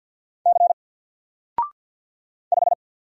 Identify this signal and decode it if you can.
Morse code, heard as a keyed tone, about 50 wpm, 700 Hz. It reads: C 5
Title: Single Letter-Number 50wpm